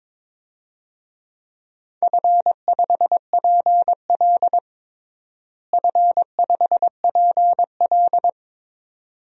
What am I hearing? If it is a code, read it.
Morse code, heard as a keyed tone, about 22 wpm, 700 Hz. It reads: F5PL F5PL
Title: DXCC-100 Call Signs - 4 Characters - Repeated 2x - 22wpm